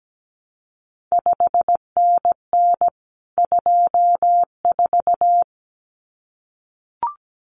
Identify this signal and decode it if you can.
Morse code, heard as a keyed tone, about 17 wpm, 700 Hz. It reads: 5NN 24
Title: CQ WW DX Contest Exchange Encore - 17wpm